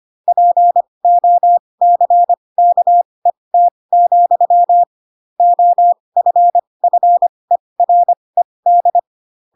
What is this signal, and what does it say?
Morse code, heard as a keyed tone, about 25 wpm, spaced as for 20 wpm, 700 Hz. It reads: POCKET, OFFERED
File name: Pinocchio / Pinocchio ch7 15 25 ik1hge